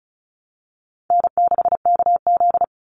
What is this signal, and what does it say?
Morse code, heard as a keyed tone, about 34 wpm, 700 Hz. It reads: N6KZ
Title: NA SOTA Chaser Callsigns 35wpm